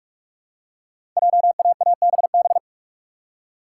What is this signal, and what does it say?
Morse code, heard as a keyed tone, about 44 wpm, 690 Hz. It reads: JAABB
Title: Code Groups - 5 Letters - 45wpm